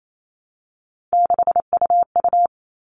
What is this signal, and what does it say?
Morse code, heard as a keyed tone, about 28 wpm, 700 Hz. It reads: 6UU
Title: All Cut Numbers - 3-Digit 28wpm